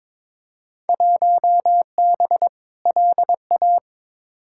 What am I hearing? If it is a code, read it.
Morse code, heard as a keyed tone, about 22 wpm, 700 Hz. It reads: 1B LA
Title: Field Day Exchanges - 22wpm